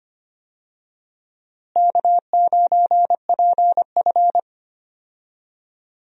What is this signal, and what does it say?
Morse code, heard as a keyed tone, about 25 wpm, 700 Hz. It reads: K9PF
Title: USA Call Signs - 4 Characters - Repeated 3x - 25wpm